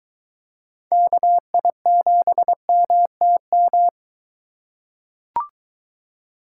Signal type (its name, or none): Morse code